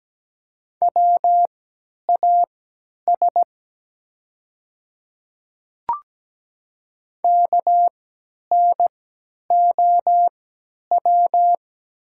Morse code, heard as a keyed tone, about 17 wpm, spaced as for 10 wpm, 700 Hz. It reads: WAS KNOW
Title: 3x Character Spacing - Top 100 Words - 17wpm